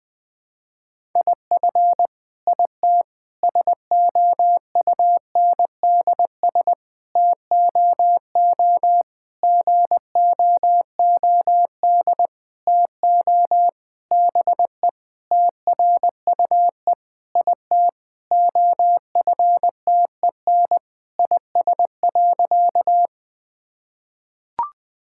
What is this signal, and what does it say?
Morse code, heard as a keyed tone, about 20 wpm, 700 Hz. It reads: IF IT SOUNDS TOO GOOD TO BE TRUE IT OFTEN IS.
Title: Sentences with 9-Word Phrases - 20wpm